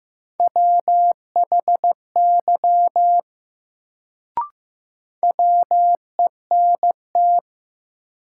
Morse code, heard as a keyed tone, about 15 wpm, 700 Hz. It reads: WHY WENT
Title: Sentences from Top 200 Words - New Word Drill - 15wpm